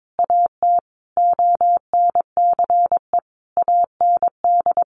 Morse code, heard as a keyed tone, about 22 wpm, 700 Hz. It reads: AT ONCE AND
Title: Sentences from Top 1100 Words 22wpm